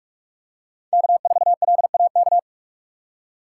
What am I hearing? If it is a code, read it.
Morse code, heard as a keyed tone, about 43 wpm, 700 Hz. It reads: K4LAK
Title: USA Call Signs - 6x Character Spacing - 45wpm